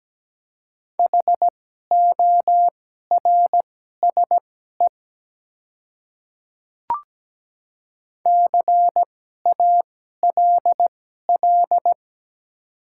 Morse code, heard as a keyed tone, about 17 wpm, spaced as for 12 wpm, 700 Hz. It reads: HORSE CALL
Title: 2x Character Spacing - Top 300 Words - 17wpm